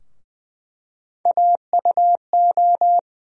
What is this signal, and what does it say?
Morse code, heard as a keyed tone, about 20 wpm, 700 Hz. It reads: AUO
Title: USA Call Sign Postfixes - 3 Letters 20wpm